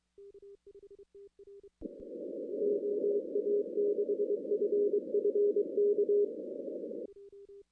Morse code, heard as a keyed tone, about 22 wpm, 410 Hz. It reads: OA6FFK
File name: 2016 arrl dx cw / 2016-ARRL-DX-CW-K5TR- - 0-26